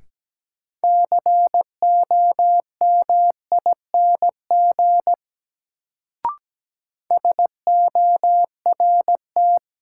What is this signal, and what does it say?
Morse code, heard as a keyed tone, about 17 wpm, 700 Hz. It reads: COMING SORT
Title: Sentences from Top 1300 Words - New Word Drill Alt - 17wpm